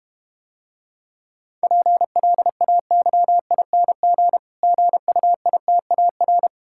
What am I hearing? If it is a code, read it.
Morse code, heard as a keyed tone, about 32 wpm, 700 Hz. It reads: PLAYING GUITAR